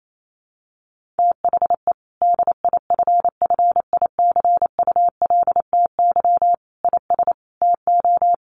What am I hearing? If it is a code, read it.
Morse code, heard as a keyed tone, about 28 wpm, 700 Hz. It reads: THE DIFFICULTY IS TO
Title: Sentences from Top 1500 Words 28wpm